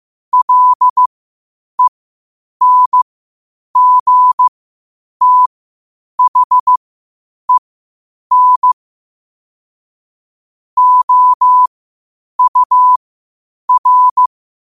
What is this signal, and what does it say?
Morse code, heard as a keyed tone, about 15 wpm, spaced as for 8 wpm, 1000 Hz. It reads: LENGTHEN OUR
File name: qotdfiles / 8wpm